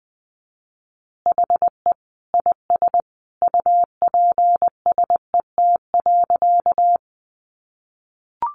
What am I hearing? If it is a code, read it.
Morse code, heard as a keyed tone, about 20 wpm, 700 Hz. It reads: HE IS UPSET.
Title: Additional Easy Phrases 20wpm